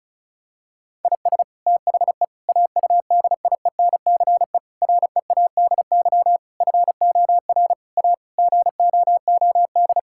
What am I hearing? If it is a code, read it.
Morse code, heard as a keyed tone, about 34 wpm, 700 Hz. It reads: IS THE AUDIENCE READY FOR A GOOD
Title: Sentences from Top 1000 Words - Review Part 5 - 35wpm